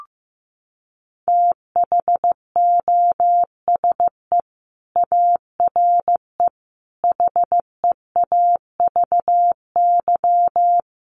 Morse code, heard as a keyed tone, about 15 wpm, 700 Hz. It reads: THOSE ARE HEAVY
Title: Sentences from Top 1100 Words 15wpm